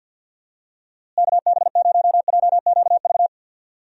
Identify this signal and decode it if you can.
Morse code, heard as a keyed tone, about 50 wpm, 700 Hz. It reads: KB0JQV
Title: USA Call Signs - 6 Characters - Repeated 3x - 50wpm